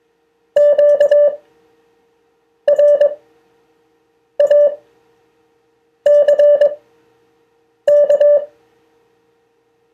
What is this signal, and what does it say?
Morse code, heard as a keyed tone, about 21 wpm, spaced as for 6 wpm, 580 Hz. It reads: QRACK
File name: 2of4-words-1